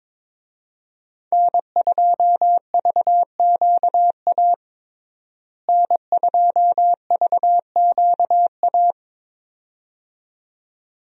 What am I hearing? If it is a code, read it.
Morse code, heard as a keyed tone, about 22 wpm, 700 Hz. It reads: N2VQA N2VQA
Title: USA Call Signs - 5 Characters - Repeated 2x - 22wpm